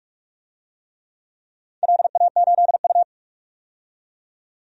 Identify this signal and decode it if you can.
Morse code, heard as a keyed tone, about 44 wpm, 700 Hz. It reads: LA8U
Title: DXCC-100 Call Signs - 4 Characters - 45wpm